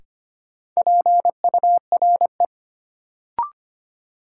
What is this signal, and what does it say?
Morse code, heard as a keyed tone, about 25 wpm, 700 Hz. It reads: PURE
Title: Sentences from Top 1500 Words - New Word Drill - 25wpm